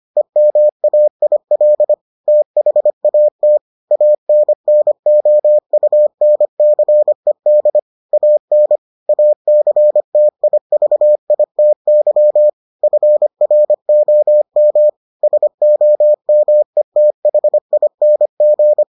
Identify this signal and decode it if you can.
Morse code, heard as a keyed tone, about 25 wpm, 600 Hz. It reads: EMAIL THAT ANNOUNCED AN ACTIVITY FROM SOMETHING